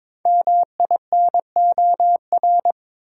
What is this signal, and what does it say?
Morse code, heard as a keyed tone, about 22 wpm, 700 Hz. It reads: MINOR
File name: 5 Letter Words - First 3 Letters Given - 22wpm